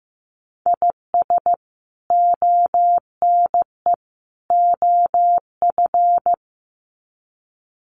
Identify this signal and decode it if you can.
Morse code, heard as a keyed tone, about 15 wpm, 700 Hz. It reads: IS ONE OF
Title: Sets of 3 Words 15wpm